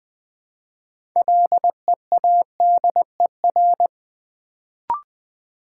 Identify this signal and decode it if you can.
Morse code, heard as a keyed tone, about 20 wpm, 700 Hz. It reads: LEADER
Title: Build A Word - 2 Letters At A Time - 6-Letter Words - 20wpm